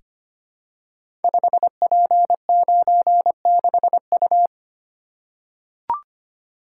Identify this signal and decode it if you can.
Morse code, heard as a keyed tone, about 25 wpm, 700 Hz. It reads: 5P96U
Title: Code Groups - 5 Letter:Numbers - 25wpm